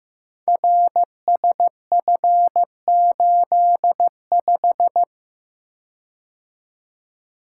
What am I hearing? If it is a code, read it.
Morse code, heard as a keyed tone, about 15 wpm, 700 Hz. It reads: RSF85
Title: Code Groups - 5 Letter:Numbers - 15wpm